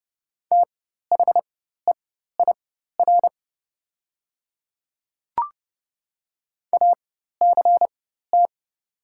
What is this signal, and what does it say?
Morse code, heard as a keyed tone, about 30 wpm, spaced as for 14 wpm, 700 Hz. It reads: THEIR ACT
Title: 4x Character Spacing - Top 200 Words - 30wpm